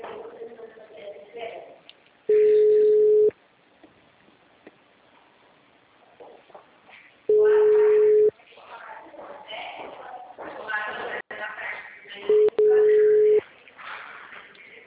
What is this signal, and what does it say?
Ringback tone (Europe, 425 Hz), with speech.